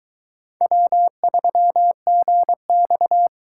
Morse code, heard as a keyed tone, about 22 wpm, 700 Hz. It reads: W3GX